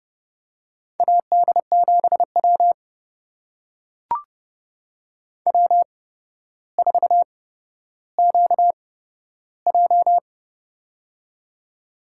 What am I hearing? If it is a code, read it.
Morse code, heard as a keyed tone, about 32 wpm, spaced as for 13 wpm, 700 Hz. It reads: AD7W W4QJ
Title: USA Call Signs - 4 Characters - 8x Character Spacing - 30wpm